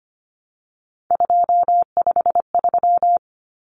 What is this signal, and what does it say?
Morse code, heard as a keyed tone, about 25 wpm, 700 Hz. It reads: 253
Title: Numbers - 3-Digit 25wpm